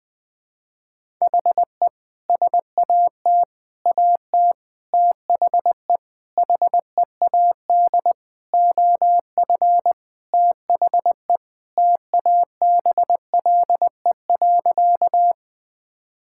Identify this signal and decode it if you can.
Morse code, heard as a keyed tone, about 20 wpm, 700 Hz. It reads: HE SAT AT THE HEAD OF THE TABLE.
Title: Sentences with 5-Word Phrases - 20wpm